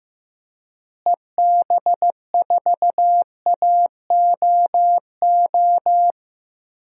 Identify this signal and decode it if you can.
Morse code, heard as a keyed tone, about 15 wpm, 700 Hz. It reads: EB4AOO
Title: DXCC-100 Call Signs - 6 Characters - Repeated 3x - 15wpm